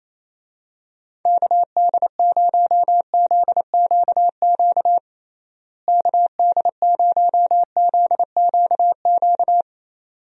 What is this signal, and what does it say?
Morse code, heard as a keyed tone, about 28 wpm, 700 Hz. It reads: KD0ZQQ KD0ZQQ
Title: USA Call Signs - 6 Characters - Repeated 2x - 28wpm